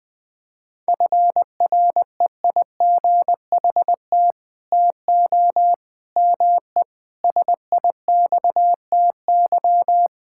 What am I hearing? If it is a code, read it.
Morse code, heard as a keyed tone, about 21 wpm, 700 Hz. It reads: FREIGHT TO ME SIXTY
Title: Pangram Challenge v2 - 20wpm